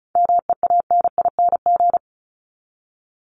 Morse code, heard as a keyed tone, about 34 wpm, 700 Hz. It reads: MEANING
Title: ING Suffix - Top 100 Words with Base Words 35wpm